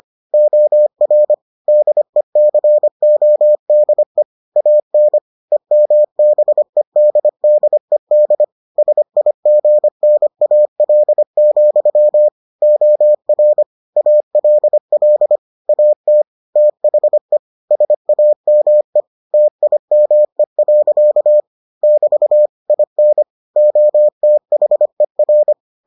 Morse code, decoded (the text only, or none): OR DECODE AN EMBEDDED SIGNAL, OR ALL AT THE SAME TIME. <BT> IN OTHER